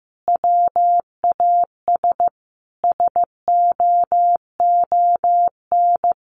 Morse code, heard as a keyed tone, about 15 wpm, 700 Hz. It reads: WAS SOON
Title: Sentences from Top 1300 Words 15wpm